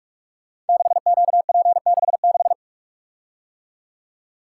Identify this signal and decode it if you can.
Morse code, heard as a keyed tone, about 44 wpm, 690 Hz. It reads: 6QP66
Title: Code Groups - 5 Letter:Numbers - 45wpm